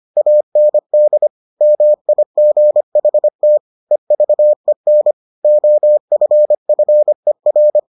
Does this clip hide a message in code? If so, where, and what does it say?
Yes: Morse code from 0.2 to 7.8 s, heard as a keyed tone, about 25 wpm, 600 Hz. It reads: AND MIGHT EVEN OFFER